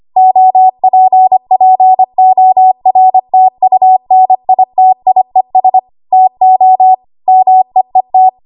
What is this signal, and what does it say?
Morse code, heard as a keyed tone, about 25 wpm, 750 Hz. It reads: OPPORTUNITIES TO MEET
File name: Archive / 25 WPM / 25WPM